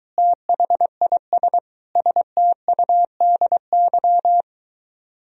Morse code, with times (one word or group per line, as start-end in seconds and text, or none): 0.2-1.6 THIS
2.0-4.4 STUDY